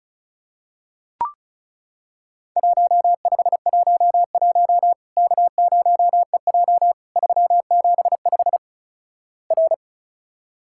Morse code, heard as a keyed tone, about 36 wpm, 700 Hz. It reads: 1511 K0EJ 375 R
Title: Worked All Europe DX Contest - QTC Exchange - 35wpm